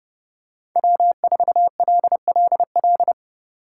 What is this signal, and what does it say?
Morse code, heard as a keyed tone, about 30 wpm, 700 Hz. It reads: W4LLL
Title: USA Call Signs - 5 Characters - 4x Character Spacing - 30wpm